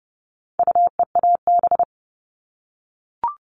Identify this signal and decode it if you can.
Morse code, heard as a keyed tone, about 29 wpm, 700 Hz. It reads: UEAB
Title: All Cut Numbers - 4-Digit 30wpm